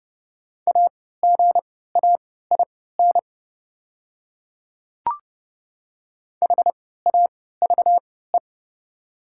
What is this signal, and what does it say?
Morse code, heard as a keyed tone, about 32 wpm, spaced as for 17 wpm, 700 Hz. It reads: AGAIN HAVE
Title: 3x Character Spacing - Top 300 Words - 30wpm